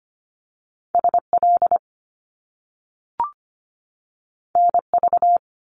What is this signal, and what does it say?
Morse code, heard as a keyed tone, about 25 wpm, 700 Hz. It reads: SL NV
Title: Top Two Letter Combinations in Words 25wpm